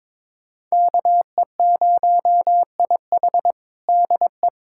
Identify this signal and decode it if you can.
Morse code, heard as a keyed tone, about 22 wpm, 700 Hz. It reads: KE0IH DE
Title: USA Callsign DE Callsign 22wpm